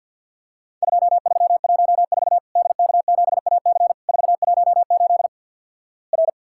Morse code, heard as a keyed tone, about 50 wpm, 700 Hz. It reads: 2314 DK7AC 418 R
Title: Worked All Europe DX Contest - QTC Exchange - 50wpm